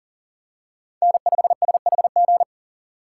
Morse code, heard as a keyed tone, about 40 wpm, 700 Hz. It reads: N5SHG
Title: USA Call Signs - 5 Characters - 2x Character Spacing - 40wpm